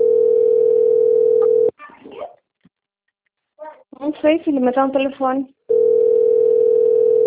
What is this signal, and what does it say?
Ringback tone (North America), 440 + 480 Hz, with speech.
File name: out- - - - - .19166